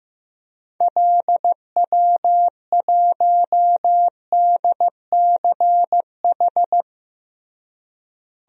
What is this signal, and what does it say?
Morse code, heard as a keyed tone, about 15 wpm, 700 Hz. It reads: LW1DCH
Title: DXCC-100 Call Signs - 6 Characters - Repeated 2x - 15wpm